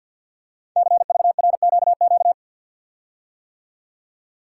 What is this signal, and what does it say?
Morse code, heard as a keyed tone, about 48 wpm, 690 Hz. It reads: CVRQQ
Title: Code Groups - 5 Letters - 50wpm